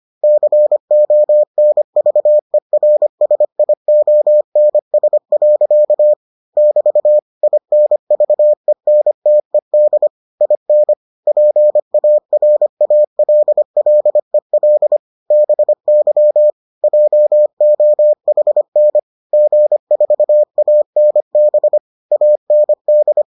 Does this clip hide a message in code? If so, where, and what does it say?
Yes: Morse code from 0.2 to 23.2 s, heard as a keyed tone, about 25 wpm, 600 Hz. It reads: CONVERSIONS. <BT> INVENTED IN PARALLEL BY JOHN G4ANB AND